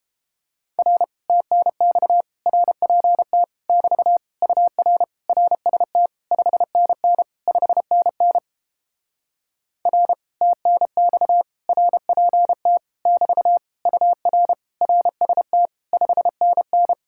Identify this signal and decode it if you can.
Morse code, heard as a keyed tone, about 29 wpm, 700 Hz. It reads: R TNX RPT <BT> UR RST 5NN 5NN R TNX RPT <BT> UR RST 5NN